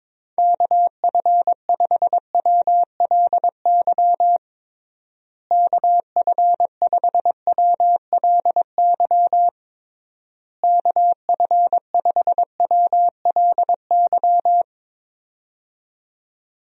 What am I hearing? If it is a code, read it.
Morse code, heard as a keyed tone, about 22 wpm, 700 Hz. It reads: KF5WLY KF5WLY KF5WLY
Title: USA Call Signs - 6 Characters - Repeated 3x - 22wpm